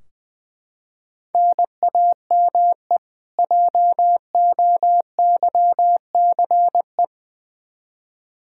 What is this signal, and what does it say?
Morse code, heard as a keyed tone, about 20 wpm, 700 Hz. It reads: NAME JOYCE
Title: K1USN SST QSO Practice 20wpm